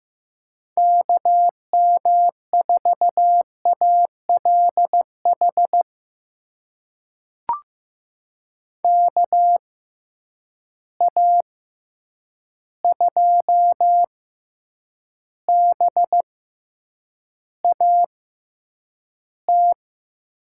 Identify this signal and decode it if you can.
Morse code, heard as a keyed tone, about 15 wpm, spaced as for 8 wpm, 700 Hz. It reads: KM4ALH KA2BAT
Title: USA Call Signs - 6 Characters - 6x Character Spacing - 15wpm